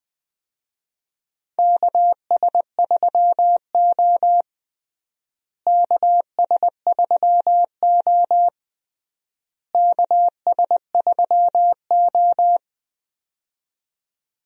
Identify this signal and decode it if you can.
Morse code, heard as a keyed tone, about 20 wpm, 700 Hz. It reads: KS3O KS3O KS3O
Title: USA Call Signs - 4 Characters - Repeated 3x - 20wpm